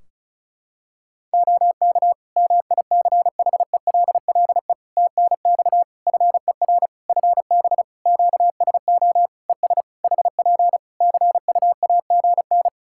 Morse code, heard as a keyed tone, about 34 wpm, 700 Hz. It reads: OK MICHELLE TNX FER FB QSO ES HP CUAGN
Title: Practice QSO - Exchange 4 - Ending - 35wpm